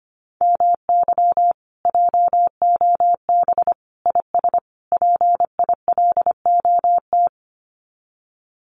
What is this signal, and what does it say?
Morse code, heard as a keyed tone, about 25 wpm, 700 Hz. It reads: MY JOB IS PILOT
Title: Common QSO Phrases Encore 25wpm